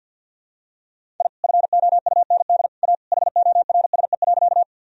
Morse code, heard as a keyed tone, about 47 wpm, 700 Hz. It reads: I FOUND A HORSE.
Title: Sentences from Top 1000 Words - Review Part 1 - 50wpm